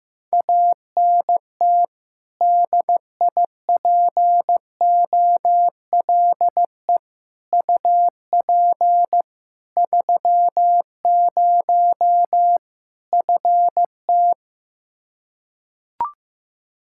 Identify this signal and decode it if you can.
Morse code, heard as a keyed tone, about 16 wpm, 700 Hz. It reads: ANT DIPOLE UP 30 FT
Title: Practice QSO - Exchange 2 - Info - 15wpm